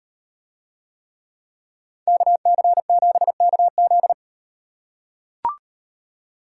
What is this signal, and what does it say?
Morse code, heard as a keyed tone, about 38 wpm, 700 Hz. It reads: KC7KZ